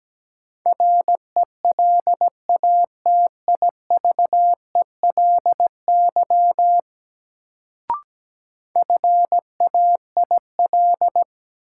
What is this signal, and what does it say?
Morse code, heard as a keyed tone, about 17 wpm, 700 Hz. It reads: RELATIVELY FAIL
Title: Sentences from Top 1100 Words - New Word Drill - 17wpm